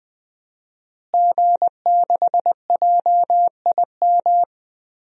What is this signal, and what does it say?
Morse code, heard as a keyed tone, about 20 wpm, 700 Hz. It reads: G6JIM
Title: DXCC-100 Call Signs - 5 Characters - Repeated 2x - 20wpm